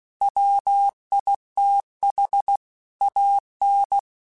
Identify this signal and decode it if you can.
Morse code, heard as a keyed tone, about 16 wpm, 800 Hz. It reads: WITH AN
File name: Technical Subjects / Technical Subjects-mp3s / Sw Pwr Sup @15